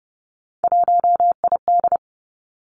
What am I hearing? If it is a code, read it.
Morse code, heard as a keyed tone, about 30 wpm, 700 Hz. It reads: 1ID
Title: Three Challenging Letter-Number Combinations 30wpm